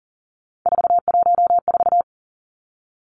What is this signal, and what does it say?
Morse code, heard as a keyed tone, about 39 wpm, 700 Hz. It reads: 414
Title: Three Challenging Letter-Number Combinations 40wpm